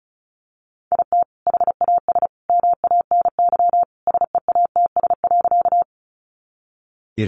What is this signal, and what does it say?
Morse code, heard as a keyed tone, about 34 wpm, 700 Hz. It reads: IT HAS MANY SEATS.
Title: Additional Easy Phrases 35wpm